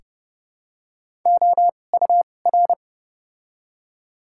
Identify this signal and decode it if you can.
Morse code, heard as a keyed tone, about 30 wpm, spaced as for 22 wpm, 700 Hz. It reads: OUR
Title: 2x Character Spacing - Top 200 Words - 30wpm